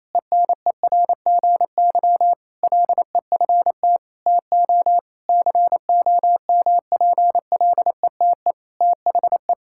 Morse code, heard as a keyed tone, about 28 wpm, 700 Hz. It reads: ENERGY LEFT TO COMPLETE THE
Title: Sentences from Top 1000 Words - Review Part 6 - 28wpm